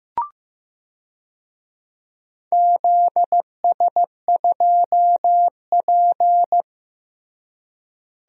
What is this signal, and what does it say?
Morse code, heard as a keyed tone, about 15 wpm, 700 Hz. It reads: ZS2P
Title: DXCC-100 Call Signs - 4 Characters - Repeated 2x - 15wpm